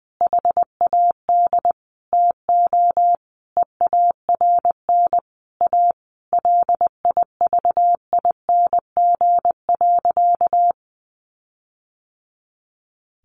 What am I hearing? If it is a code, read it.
Morse code, heard as a keyed tone, about 20 wpm, 700 Hz. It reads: HAD TO EARN A LIVING.
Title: Sentences from Top 1500 Words 20wpm